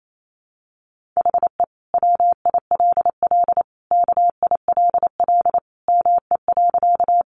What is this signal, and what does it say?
Morse code, heard as a keyed tone, about 28 wpm, 700 Hz. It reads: HE WILL KILL ME.
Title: Sentences from Top 1400 Words 28wpm